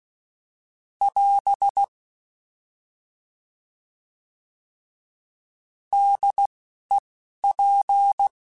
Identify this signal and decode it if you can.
Morse code, heard as a keyed tone, about 16 wpm, 800 Hz. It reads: <AS> D E P
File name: Alphabetical / Alphabetical mp3s / Ds @10